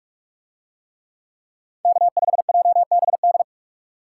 Morse code, heard as a keyed tone, about 44 wpm, 700 Hz. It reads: K5JBD